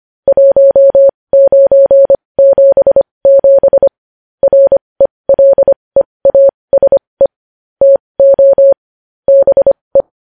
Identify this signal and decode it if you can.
Morse code, heard as a keyed tone, about 25 wpm, spaced as for 20 wpm, 550 Hz. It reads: 1977 RELEASE TO BE
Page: news headlines in morse code at 20 wpm